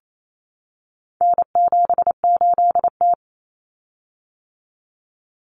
Morse code, heard as a keyed tone, about 28 wpm, 700 Hz. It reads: N78T
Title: Common Cut Numbers - 4-Digit 28wpm